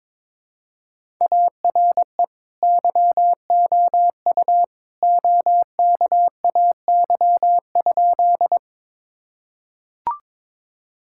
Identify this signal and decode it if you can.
Morse code, heard as a keyed tone, about 22 wpm, 700 Hz. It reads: ARE YOU OKAY?
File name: practice / 22wpm / Sentences with 2-Word Phrases - 22wpm